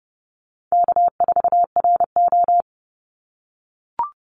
Morse code, heard as a keyed tone, about 30 wpm, 700 Hz. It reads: K4RO
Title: Call Signs - 30wpm